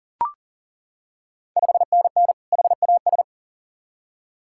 Morse code, heard as a keyed tone, about 40 wpm, 690 Hz. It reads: 5NN HAS
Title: Ontario QSO Party Exchange 40wpm